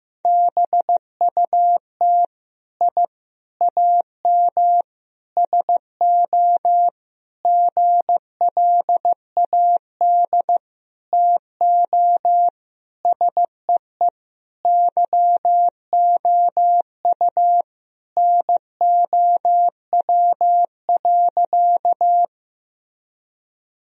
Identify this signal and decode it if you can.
Morse code, heard as a keyed tone, about 15 wpm, 700 Hz. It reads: BUT I AM SO GLAD TO SEE YOU NOW.
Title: Sentences from Top 1000 Words - Review Part 2 - 15wpm